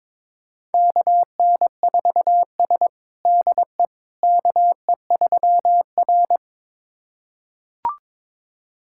Morse code, heard as a keyed tone, about 22 wpm, 700 Hz. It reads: KN4S DE KE3R
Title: USA Callsign DE Callsign - 4 Characters - 22wpm